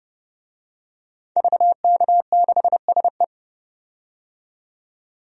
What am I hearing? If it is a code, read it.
Morse code, heard as a keyed tone, about 32 wpm, 700 Hz. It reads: VK6SE
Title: DXCC-100 Call Signs - 5 Characters - 30wpm